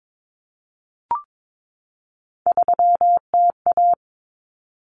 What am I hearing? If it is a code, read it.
Morse code, heard as a keyed tone, about 22 wpm, 700 Hz. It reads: 3TA
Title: Common Cut Numbers - 3-Digit 22wpm